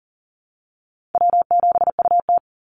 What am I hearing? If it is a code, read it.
Morse code, heard as a keyed tone, about 40 wpm, 700 Hz. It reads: W7UT
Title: Call Signs Repeated 2x - 40wpm